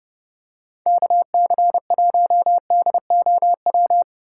Morse code, heard as a keyed tone, about 30 wpm, 700 Hz. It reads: KC1DOW
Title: USA Call Signs - 2x Character Spacing - 30wpm